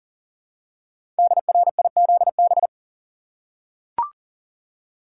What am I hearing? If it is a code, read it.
Morse code, heard as a keyed tone, about 40 wpm, 700 Hz. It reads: DRIZB
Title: Code Groups - 5 Letters - 40wpm